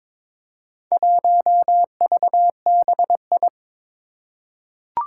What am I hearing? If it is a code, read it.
Morse code, heard as a keyed tone, about 22 wpm, 700 Hz. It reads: 1VBI
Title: Four Challenging Letter-Number Combinations 22wpm